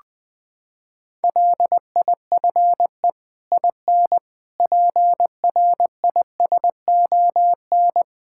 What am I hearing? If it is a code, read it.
Morse code, heard as a keyed tone, about 20 wpm, 700 Hz. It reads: LIFE IN PRISON
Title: Sets of 8 Words - 20wpm